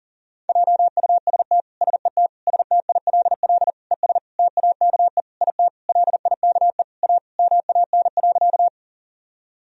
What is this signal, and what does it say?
Morse code, heard as a keyed tone, about 40 wpm, 700 Hz. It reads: JUST SET STILL ES TAKE IT LIKE A MAN.
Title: Sentences from Top 300 Words and Abbreviations 40wpm